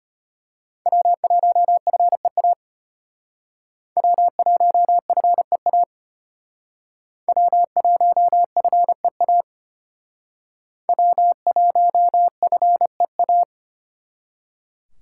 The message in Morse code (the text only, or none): W1FEA W1FEA W1FEA W1FEA